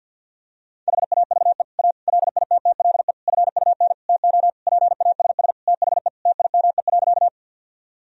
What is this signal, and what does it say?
Morse code, heard as a keyed tone, about 50 wpm, 700 Hz. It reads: HAVE A LITTLE FUN TO PASS THE TIME.
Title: Sentences from Top 1000 Words - Review Part 4 - 50wpm